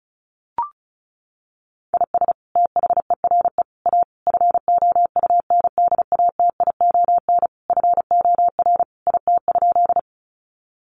Morse code, heard as a keyed tone, about 34 wpm, 700 Hz. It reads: IS THERE A FOUNDATION FOR IT?
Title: Sentences from Top 1200 Words 35wpm